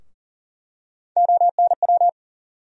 Morse code, heard as a keyed tone, about 40 wpm, 700 Hz. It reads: ONW
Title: USA Call Sign Postfixes - 3 Letters 40wpm